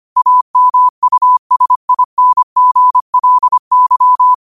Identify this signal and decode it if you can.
Morse code, heard as a keyed tone, about 25 wpm, 1000 Hz. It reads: AMUSINGLY
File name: qotdfiles / 25wpm